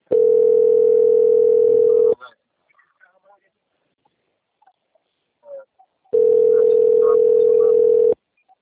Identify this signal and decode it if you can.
Ringback tone (North America), 440 + 480 Hz, with speech.
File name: out- - - - - .3813